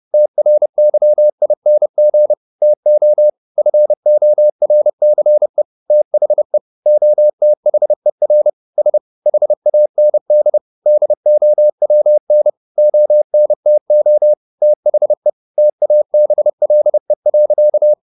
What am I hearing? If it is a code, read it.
Morse code, heard as a keyed tone, about 30 wpm, 600 Hz. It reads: TRYING TO FORCE THE OTHER S HAND DOWN ONTO THE TABLE.
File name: the old man and the sea / the old man and the sea 1 30wpm